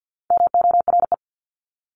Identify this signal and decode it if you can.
Morse code, heard as a keyed tone, about 50 wpm, 700 Hz. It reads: MORE
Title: Top 500 Words 50wpm